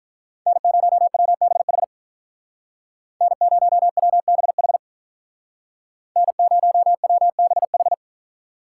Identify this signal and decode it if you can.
Morse code, heard as a keyed tone, about 47 wpm, 700 Hz. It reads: N0WBH N0WBH N0WBH